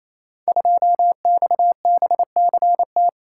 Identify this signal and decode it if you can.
Morse code, heard as a keyed tone, about 28 wpm, 700 Hz. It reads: 2XBCT
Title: Code Groups - 5 Letter:Numbers - 28wpm